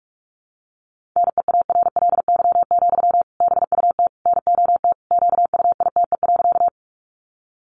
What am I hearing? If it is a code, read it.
Morse code, heard as a keyed tone, about 44 wpm, 700 Hz. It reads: NEARLY, BUT NOT QUITE.
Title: Sentences from Top 1300 Words 45wpm